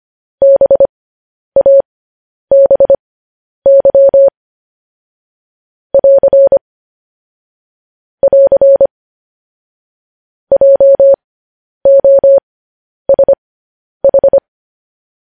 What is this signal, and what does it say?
Morse code, heard as a keyed tone, about 25 wpm, spaced as for 10 wpm, 550 Hz. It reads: BABY <AR> <AR> JOSH
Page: news-headlines-in-morse-code-at-10-wpm